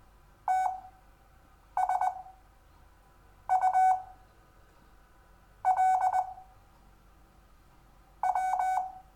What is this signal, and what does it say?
Morse code, heard as a keyed tone, about 20 wpm, spaced as for 6 wpm, 750 Hz. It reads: TSU L W
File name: cw / Lesson6 WI